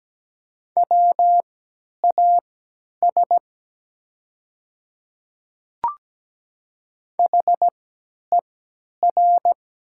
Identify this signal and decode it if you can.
Morse code, heard as a keyed tone, about 17 wpm, spaced as for 10 wpm, 700 Hz. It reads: WAS HER
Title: 3x Character Spacing - Top 100 Words - 17wpm